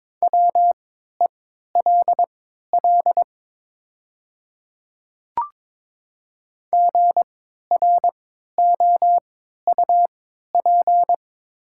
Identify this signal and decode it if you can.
Morse code, heard as a keyed tone, about 22 wpm, spaced as for 12 wpm, 700 Hz. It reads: WELL GROUP
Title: 3x Character Spacing - Top 500 Words - 22wpm